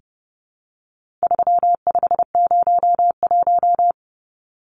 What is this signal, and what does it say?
Morse code, heard as a keyed tone, about 30 wpm, 700 Hz. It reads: 3501
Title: Numbers - 4-Digit 30wpm